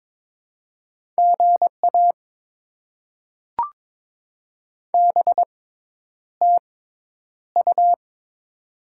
Morse code, heard as a keyed tone, about 22 wpm, spaced as for 10 wpm, 700 Hz. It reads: GA BTU
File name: Top 20 QSO Elements - 6x Character Spacing - 22wpm